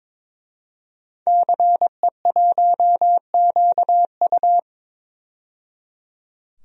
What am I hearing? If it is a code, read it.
Morse code, heard as a keyed tone, about 21 wpm, 700 Hz. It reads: CE1QU
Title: DXCC-100 Call Signs - 5 Characters - 22wpm